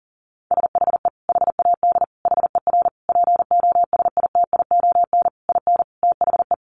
Morse code, heard as a keyed tone, about 40 wpm, 700 Hz. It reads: SHE HAD HER POSITION IN THE
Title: Sentences from Top 900 Words 40wpm